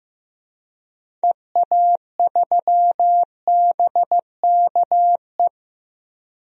Morse code, heard as a keyed tone, about 15 wpm, 700 Hz. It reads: EA3BKE
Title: DXCC-100 Call Signs - 6 Characters - Repeated 3x - 15wpm